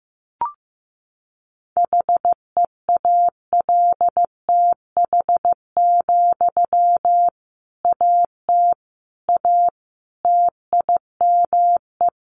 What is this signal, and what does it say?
Morse code, heard as a keyed tone, about 15 wpm, 700 Hz. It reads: HEALTH, AT A TIME
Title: Sentences from Top 600 Words 15wpm